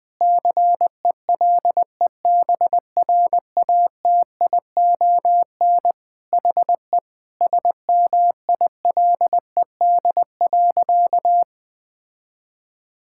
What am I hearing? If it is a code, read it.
Morse code, heard as a keyed tone, about 20 wpm, 700 Hz. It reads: CELEBRATION HE SMILED.
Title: Sentences with 5-Word Phrases - Spoken Beginning - 20wpm